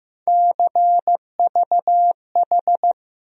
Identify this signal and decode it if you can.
Morse code, heard as a keyed tone, about 15 wpm, 700 Hz. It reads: CVH
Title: USA Call Sign Postfixes - 3 Letters 15wpm